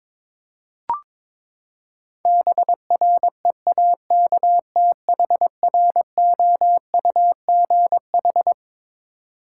Morse code, heard as a keyed tone, about 21 wpm, 700 Hz. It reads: BREAKTHROUGH
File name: Twelve Letter Words - 22wpm